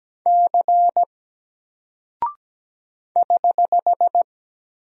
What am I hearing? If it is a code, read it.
Morse code, heard as a keyed tone, about 17 wpm, 700 Hz. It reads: C <HH>
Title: Single Character (Letter:Number:Punctuation:Prosign) Alt 17wpm